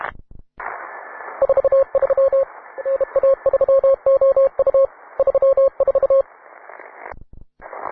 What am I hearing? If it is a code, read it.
Morse code, heard as a keyed tone, about 31 wpm, 540 Hz. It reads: 43 NA3OU 34